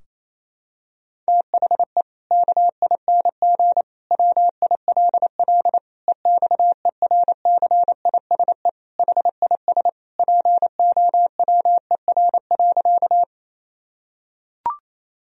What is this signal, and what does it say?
Morse code, heard as a keyed tone, about 28 wpm, 700 Hz. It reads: THE KING WILL EXERCISE HIS POWER.
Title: Sentences from Top 1000 Words - Review Part 4 - 28wpm